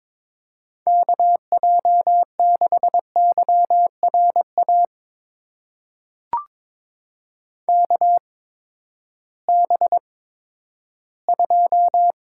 Morse code, heard as a keyed tone, about 22 wpm, spaced as for 13 wpm, 700 Hz. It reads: KJ6YRA KB2
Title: USA Call Signs - 6 Characters - 8x Character Spacing - 22wpm